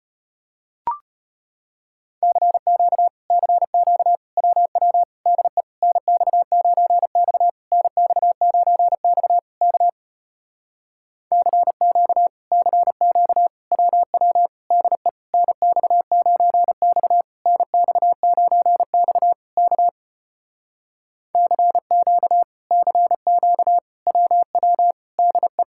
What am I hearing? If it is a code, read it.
Morse code, heard as a keyed tone, about 33 wpm, 700 Hz. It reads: CQ CQ WW DE NX9X NX9X K CQ CQ WW DE NX9X NX9X K CQ CQ WW DE